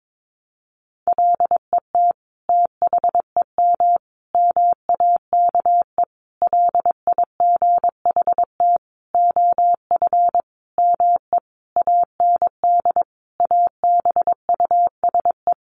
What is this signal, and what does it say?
Morse code, heard as a keyed tone, about 22 wpm, 700 Hz. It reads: LET THEM MAKE LIGHT OF ME AND ABUSE
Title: Sentences from Top 1300 Words 22wpm